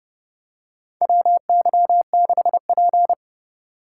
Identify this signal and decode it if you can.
Morse code, heard as a keyed tone, about 30 wpm, 700 Hz. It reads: WY6P
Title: USA Call Signs - 4 Characters - 2x Character Spacing - 30wpm